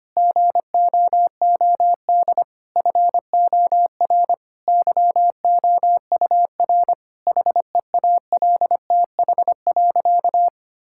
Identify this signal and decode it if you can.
Morse code, heard as a keyed tone, about 25 wpm, 700 Hz. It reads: GOOD FOR YOUR HEALTH.
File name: practice / 25wpm / Difficult Sentences with Hints - 25wpm